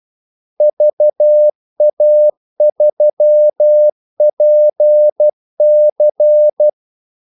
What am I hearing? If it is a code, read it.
Morse code, heard as a keyed tone, about 12 wpm, 600 Hz. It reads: VA3PC
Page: Kaldesignaler 12 wpm